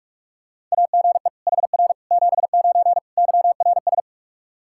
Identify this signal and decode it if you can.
Morse code, heard as a keyed tone, about 44 wpm, 700 Hz. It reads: AGE HR 79 YRS
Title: Practice QSO - Exchange 3 - More Info - 45wpm